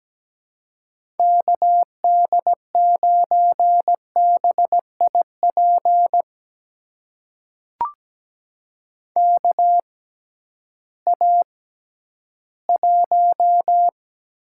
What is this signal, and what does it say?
Morse code, heard as a keyed tone, about 17 wpm, spaced as for 11 wpm, 700 Hz. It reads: KD9BIP KA1
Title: USA Call Signs - 6 Characters - 6x Character Spacing - 17wpm